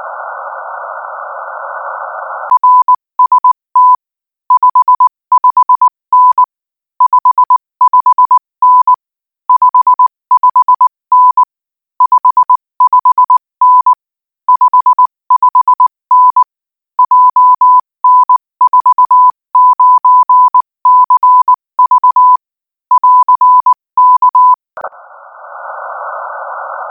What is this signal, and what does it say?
Morse code, heard as a keyed tone, about 19 wpm, 1000 Hz. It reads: RST 55N 55N 55N 55N 55N JN49CV <AR> K
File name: IZ4BEH-13-cm-AS